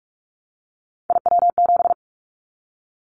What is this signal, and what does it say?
Morse code, heard as a keyed tone, about 44 wpm, 700 Hz. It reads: IW7
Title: Callsign Prefixes for Top 25 DXCC - Repeated 3x - 45wpm